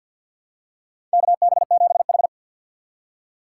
Morse code, heard as a keyed tone, about 50 wpm, 700 Hz. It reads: KB7H
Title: K1USN SST QSO Practice - 50wpm